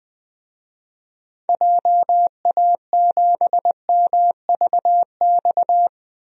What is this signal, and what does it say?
Morse code, heard as a keyed tone, about 20 wpm, 700 Hz. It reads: JA7MVX
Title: DXCC-100 Call Signs - 6 Characters - Repeated 3x - 20wpm